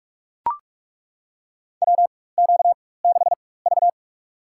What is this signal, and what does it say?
Morse code, heard as a keyed tone, about 41 wpm, spaced as for 20 wpm, 700 Hz. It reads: WQ6V
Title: USA Call Signs - 4 Characters - 4x Character Spacing - 45wpm